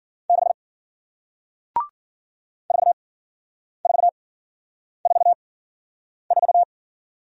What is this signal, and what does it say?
Morse code, heard as a keyed tone, about 54 wpm, spaced as for 9 wpm, 700 Hz. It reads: 6 4444